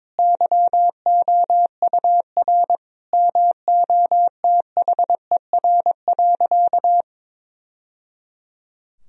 Morse code, heard as a keyed tone, about 22 wpm, 700 Hz. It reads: YOUR MOTHER.
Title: Sentences from Top 1000 Words - Review Part 2 - 22wpm